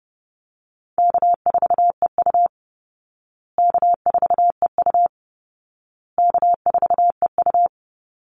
Morse code, heard as a keyed tone, about 30 wpm, 700 Hz. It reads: K4EU K4EU K4EU
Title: Easy Call Signs Repeated 3x - 30wpm